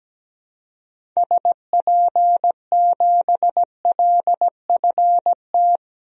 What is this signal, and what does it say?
Morse code, heard as a keyed tone, about 17 wpm, 700 Hz. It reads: SP7LFT
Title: DXCC-100 Call Signs - 6 Characters - 17wpm